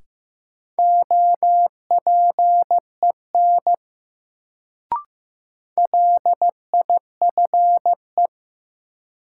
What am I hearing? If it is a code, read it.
Morse code, heard as a keyed tone, about 15 wpm, 700 Hz. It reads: OPEN LIFE
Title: Sentences from Top 300 Words - New Word Drill - 15wpm